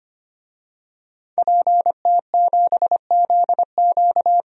Morse code, heard as a keyed tone, about 25 wpm, 700 Hz. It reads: PT7ZQ
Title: DXCC-100 Call Signs - 5 Characters - 25wpm